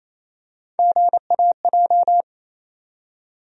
Morse code, heard as a keyed tone, about 28 wpm, 700 Hz. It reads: GAJ
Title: USA Call Sign Postfixes - 3 Letters 28wpm